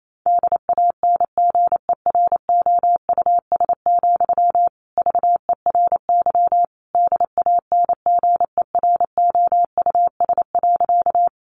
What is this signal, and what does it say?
Morse code, heard as a keyed tone, about 28 wpm, 700 Hz. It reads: DANGEROUS, VERY DANGEROUS.
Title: Sentences from Top 1200 Words 28wpm